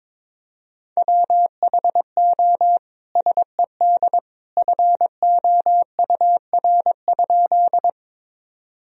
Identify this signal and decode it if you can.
Morse code, heard as a keyed tone, about 22 wpm, 700 Hz. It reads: WHO SED FOUR?
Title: Sentences from Top 300 Words and Abbreviations 22wpm